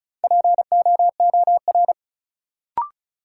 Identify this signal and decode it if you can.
Morse code, heard as a keyed tone, about 34 wpm, 700 Hz. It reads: POOR
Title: Sentences from Top 1000 Words - New Word Drill - 35wpm